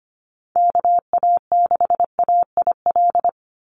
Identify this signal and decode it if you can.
Morse code, heard as a keyed tone, about 25 wpm, 700 Hz. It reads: KA6AIL
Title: Call Signs - 25wpm